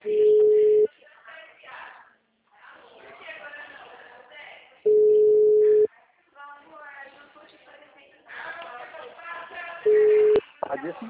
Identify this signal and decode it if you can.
Ringback tone (Europe, 425 Hz), with speech.